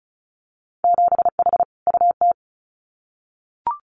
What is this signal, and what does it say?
Morse code, heard as a keyed tone, about 36 wpm, 700 Hz. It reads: 7H UT
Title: Winter Field Day Exchanges Encore 35wpm